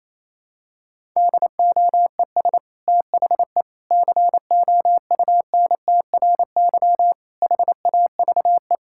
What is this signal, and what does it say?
Morse code, heard as a keyed tone, about 28 wpm, 700 Hz. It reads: DOES THE COUNTRY HAVE
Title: Sentences from Top 1000 Words - Review Part 5 - 28wpm